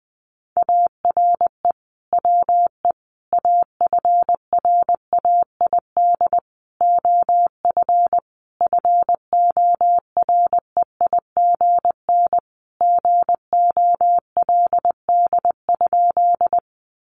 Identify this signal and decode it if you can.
Morse code, heard as a keyed tone, about 20 wpm, 700 Hz. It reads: ARE WE AFRAID OF FOREIGN GOLD?
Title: Sentences from Top 1500 Words 20wpm